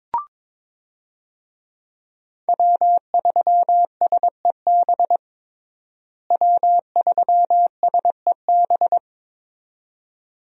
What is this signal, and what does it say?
Morse code, heard as a keyed tone, about 22 wpm, 700 Hz. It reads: W3SEB W3SEB
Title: USA Call Signs - 5 Characters - Repeated 2x - 22wpm